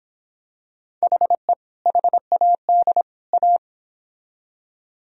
Morse code, heard as a keyed tone, about 26 wpm, 700 Hz. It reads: HE HAD A